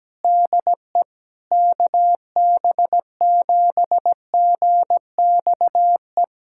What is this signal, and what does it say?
Morse code, heard as a keyed tone, about 17 wpm, 700 Hz. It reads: DE KB7GXE
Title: USA Callsign DE Callsign - 6 Characters - 17wpm